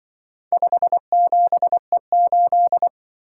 Morse code, heard as a keyed tone, about 24 wpm, 700 Hz. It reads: 57E8
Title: Speed-Racing - Four Letter-Number Combinations - 20wpm